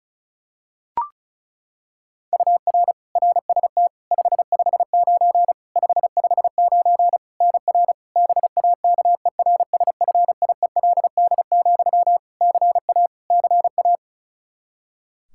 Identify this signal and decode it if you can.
Morse code, heard as a keyed tone, about 34 wpm, 700 Hz. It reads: UR RST 559 559 NR BAKERSFIELD, CA CA
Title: Practice QSO - Exchange 1 - Beginning - 35wpm